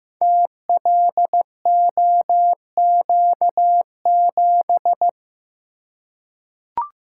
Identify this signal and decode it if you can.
Morse code, heard as a keyed tone, about 15 wpm, 700 Hz. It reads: TLOQ7
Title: Code Groups - 5 Letter:Numbers - 15wpm